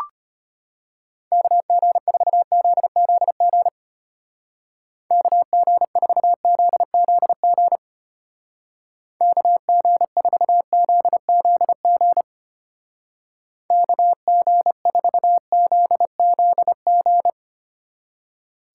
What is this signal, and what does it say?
Morse code, heard as a keyed tone, about 31 wpm, 700 Hz. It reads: KG4ZZG KG4ZZG KG4ZZG KG4ZZG
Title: Speed Racing - USA 2x3 Call Signs 25wpm